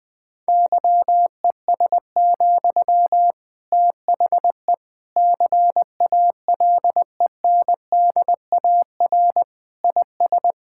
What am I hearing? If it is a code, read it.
Morse code, heard as a keyed tone, about 19 wpm, 700 Hz. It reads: YES, THE CALENDAR IS
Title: Sentences from Top 1600 Words 20wpm